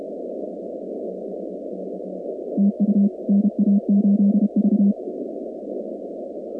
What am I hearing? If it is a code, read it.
Morse code, heard as a keyed tone, about 32 wpm, 200 Hz. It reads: TU NA8V
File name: contests / 2019-cw-wpx-kt5j-k5pi- -2330 0-46